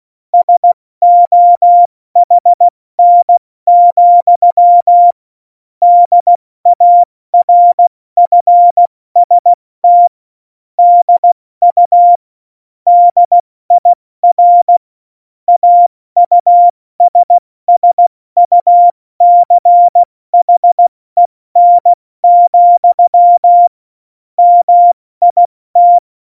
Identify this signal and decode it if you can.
Morse code, heard as a keyed tone, about 16 wpm, 700 Hz. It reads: SOHN, DARFST DU DIR AUSSUCHEN, MIT
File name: Morsekurs / Witze / Witz 26. Januar 2021, 70 F80